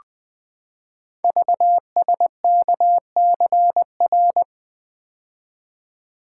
Morse code, heard as a keyed tone, about 20 wpm, 700 Hz. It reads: VSKCR
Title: Code Groups - 5 Letters - 20wpm